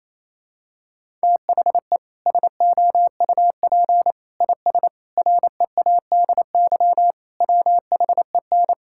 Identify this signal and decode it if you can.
Morse code, heard as a keyed tone, about 28 wpm, 700 Hz. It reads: THE SOUP IS READY WHEN
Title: Sentences from Top 1600 Words 28wpm